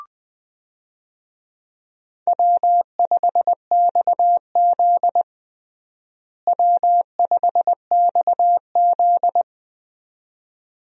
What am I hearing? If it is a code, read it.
Morse code, heard as a keyed tone, about 20 wpm, 700 Hz. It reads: W5XZ W5XZ
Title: USA Call Signs - 4 Characters - Repeated 2x - 20wpm